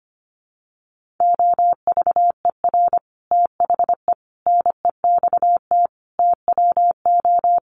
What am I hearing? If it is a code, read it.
Morse code, heard as a keyed tone, about 25 wpm, 700 Hz. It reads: OVER THE NEXT TWO
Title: Sets of 5 Words Encore2 25wpm